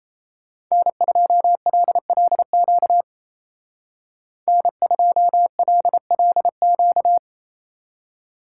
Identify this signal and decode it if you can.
Morse code, heard as a keyed tone, about 32 wpm, 700 Hz. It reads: N2LLQ N2LLQ
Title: Speed Racing - USA 1x3 Call Signs 28wpm